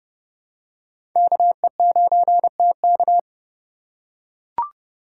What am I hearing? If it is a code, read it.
Morse code, heard as a keyed tone, about 30 wpm, 700 Hz. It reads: KE9TK
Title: USA Call Signs - 2x Character Spacing - 30wpm